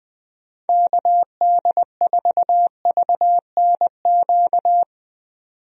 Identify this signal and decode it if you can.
Morse code, heard as a keyed tone, about 20 wpm, 700 Hz. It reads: KD4VNQ
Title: USA Call Signs - 6 Characters - 20wpm